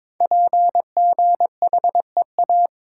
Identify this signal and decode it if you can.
Morse code, heard as a keyed tone, about 22 wpm, 700 Hz. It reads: PGHEA
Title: Code Groups - 5 Letters - 22wpm